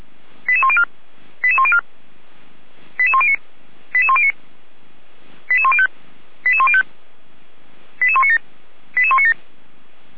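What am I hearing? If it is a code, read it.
Five-tone selective call (ZVEI), heard as quick runs of five tones, tones of about 70 ms. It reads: 88104 88104 88109 88109 88105 88105 88197 88197
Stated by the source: ZVEI five-tone selective call received by SDR, from feuerwehr-orlamuende.org fire brigade audio archive.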